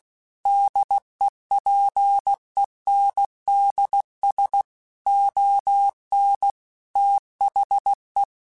Morse code, heard as a keyed tone, about 16 wpm, 800 Hz. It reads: DEPENDS ON THE
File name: Technical Subjects / Technical Subjects-mp3s / Sw Pwr Sup @15